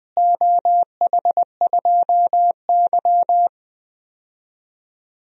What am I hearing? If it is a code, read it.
Morse code, heard as a keyed tone, about 20 wpm, 700 Hz. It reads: OH2Y
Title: DXCC-100 Call Signs - 4 Characters - Repeated 2x - 20wpm